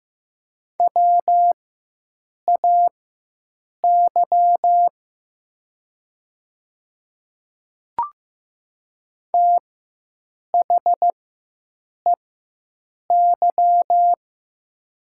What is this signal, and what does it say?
Morse code, heard as a keyed tone, about 15 wpm, spaced as for 7 wpm, 700 Hz. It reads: WAY THEY
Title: 4x Character Spacing - Top 100 Words - 15wpm